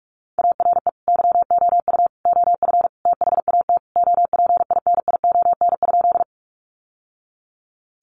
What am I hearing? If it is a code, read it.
Morse code, heard as a keyed tone, about 44 wpm, 700 Hz. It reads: ARE YOU OF THAT OPINION?
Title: Sentences from Top 1200 Words 45wpm